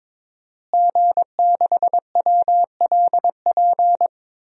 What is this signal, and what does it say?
Morse code, heard as a keyed tone, about 22 wpm, 700 Hz. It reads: G6WLP
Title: DXCC-100 Call Signs - 5 Characters - Repeated 2x - 22wpm